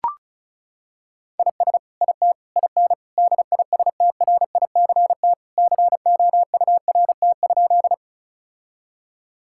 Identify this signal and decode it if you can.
Morse code, heard as a keyed tone, about 34 wpm, 700 Hz. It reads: IS IT IN DISTRICT COURT?
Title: Sentences from Top 1600 Words 35wpm